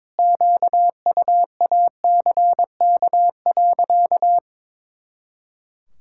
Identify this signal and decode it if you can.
Morse code, heard as a keyed tone, about 22 wpm, 700 Hz. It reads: QUACK.
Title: Pangram Challenge v2 - 22wpm